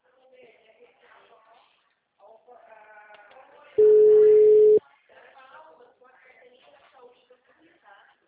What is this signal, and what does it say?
Ringback tone (Europe, 425 Hz), with speech.